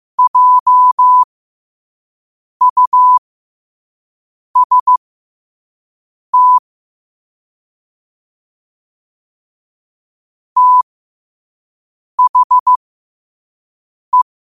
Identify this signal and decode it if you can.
Morse code, heard as a keyed tone, about 15 wpm, spaced as for 5 wpm, 1000 Hz. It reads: JUST THE
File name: qotdfiles / 5wpm